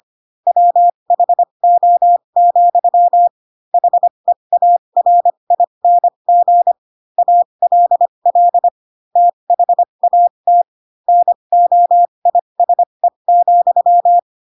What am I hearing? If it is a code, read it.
Morse code, heard as a keyed tone, about 25 wpm, spaced as for 22 wpm, 700 Hz. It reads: WHO, HEARING ALL THAT NOISE,
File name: Pinocchio / Pinocchio ch3 18 25 ik1hge